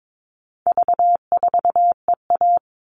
Morse code, heard as a keyed tone, about 22 wpm, 700 Hz. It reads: V4EA
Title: All Cut Numbers - 4-Digit 22wpm